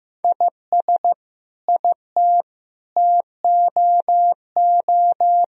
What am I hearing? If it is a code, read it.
Morse code, heard as a keyed tone, about 15 wpm, 700 Hz. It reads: IS IT TOO
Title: Sentences from Top 1000 Words - Review Part 2 - 15wpm